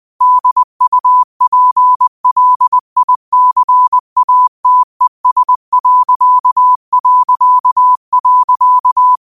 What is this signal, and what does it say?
Morse code, heard as a keyed tone, about 20 wpm, 1000 Hz. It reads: DUPLICATES...
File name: qotdfiles / 20wpm